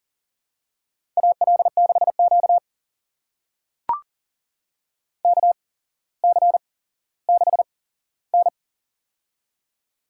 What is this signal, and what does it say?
Morse code, heard as a keyed tone, about 38 wpm, spaced as for 17 wpm, 700 Hz. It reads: AL6Q KC6N
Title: USA Call Signs - 4 Characters - 8x Character Spacing - 40wpm